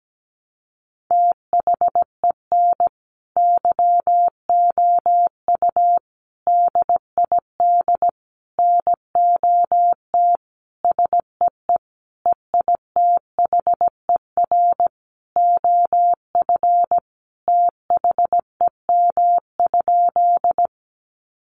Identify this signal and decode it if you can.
Morse code, heard as a keyed tone, about 17 wpm, 700 Hz. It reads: THEN YOU DID NOT SEE EITHER OF THEM?
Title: Sentences from Top 600 Words 17wpm